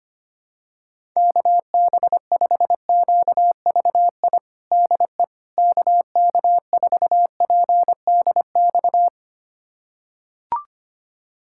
Morse code, heard as a keyed tone, about 25 wpm, 700 Hz. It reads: KB5QVI DE KK4PDX
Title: USA Callsign DE Callsign - 6 Characters - 25wpm